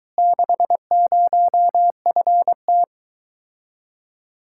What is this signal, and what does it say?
Morse code, heard as a keyed tone, about 22 wpm, 700 Hz. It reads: 60FT